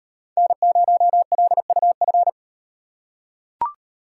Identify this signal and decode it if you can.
Morse code, heard as a keyed tone, about 38 wpm, 700 Hz. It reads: N0LUF